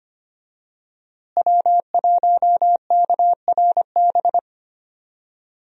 Morse code, heard as a keyed tone, about 25 wpm, 700 Hz. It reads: W1KRB
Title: USA Call Signs - 5 Characters - Repeated 2x - 25wpm